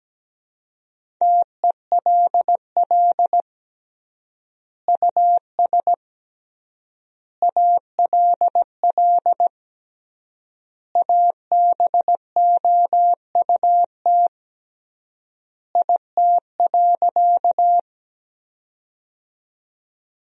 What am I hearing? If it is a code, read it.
Morse code, heard as a keyed tone, about 17 wpm, 700 Hz. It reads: TELL US ALL ABOUT IT.
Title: 3x Word-Spacing - Sentences from Top 200 Words - 17wpm